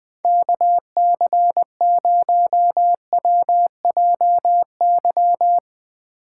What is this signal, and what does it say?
Morse code, heard as a keyed tone, about 20 wpm, 700 Hz. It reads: KC0WJY
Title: Speed Racing - USA 2x3 Call Signs 20wpm